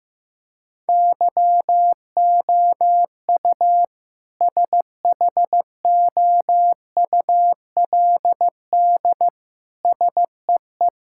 Morse code, heard as a keyed tone, about 15 wpm, 700 Hz. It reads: YOU SHOULD SEE
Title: Sentences from Top 1600 Words 15wpm